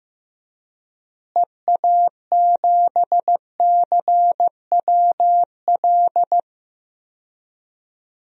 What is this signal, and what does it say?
Morse code, heard as a keyed tone, about 15 wpm, 700 Hz. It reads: EA7CWL
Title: DXCC-100 Call Signs - 6 Characters - Repeated 3x - 15wpm